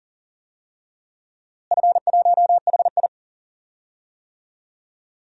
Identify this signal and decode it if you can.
Morse code, heard as a keyed tone, about 38 wpm, 700 Hz. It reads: F1HI
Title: DXCC-100 Call Signs - 4 Characters - Repeated 3x - 40wpm